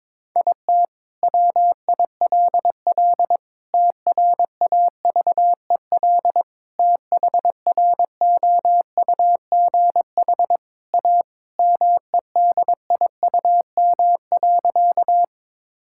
Morse code, heard as a keyed tone, about 22 wpm, 700 Hz. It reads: IT WILL TRAVEL THROUGH A MEDIUM.
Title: Sentences from Top 1000 Words - Review Part 6 - 22wpm